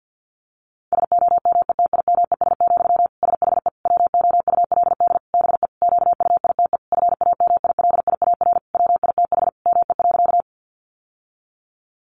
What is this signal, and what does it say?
Morse code, heard as a keyed tone, about 48 wpm, 700 Hz. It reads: SOMETIMES, SHE WOULD BE QUITE FAMILIAR WITH ME.
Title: Sentences from Top 1000 Words 50wpm